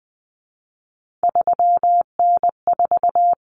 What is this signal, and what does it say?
Morse code, heard as a keyed tone, about 20 wpm, 700 Hz. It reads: 3N4
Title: Common Cut Numbers - 3-Digit 20wpm